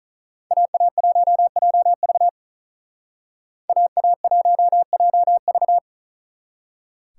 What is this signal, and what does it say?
Morse code, heard as a keyed tone, about 46 wpm, 700 Hz. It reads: AA1JV AA1JV